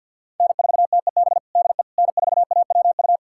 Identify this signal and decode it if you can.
Morse code, heard as a keyed tone, about 51 wpm, 690 Hz. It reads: N4TEB DE N4AWU
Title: USA Callsign DE Callsign - 5 Characters - 50wpm